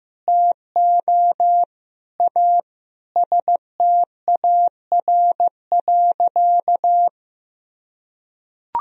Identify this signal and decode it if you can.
Morse code, heard as a keyed tone, about 15 wpm, 700 Hz. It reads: TO A STAR.
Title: Sentences from Top 1000 Words - Review Part 2 - 15wpm